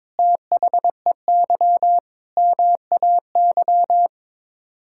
Morse code, heard as a keyed tone, about 22 wpm, 700 Hz. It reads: THEY MAY
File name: practice / 22wpm / Sets of 8 Words - 22wpm